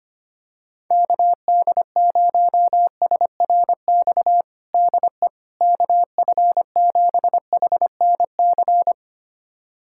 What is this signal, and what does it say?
Morse code, heard as a keyed tone, about 25 wpm, 700 Hz. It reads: KD0SRX DE KF7HNC
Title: USA Callsign DE Callsign - 6 Characters - 25wpm